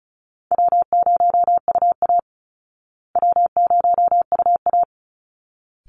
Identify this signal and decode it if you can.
Morse code, heard as a keyed tone, about 34 wpm, 700 Hz. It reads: W0UA W0UA
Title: Easy Call Signs Repeated 2x - 35wpm